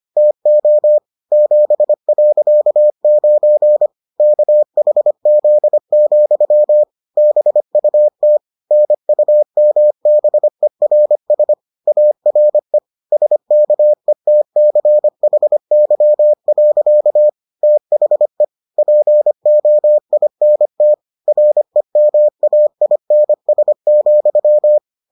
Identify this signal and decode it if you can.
Morse code, heard as a keyed tone, about 25 wpm, 600 Hz. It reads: TO 7.9 KHZ, BUT NUMBERS ARE SKETCHY. THE POINT REMAINS,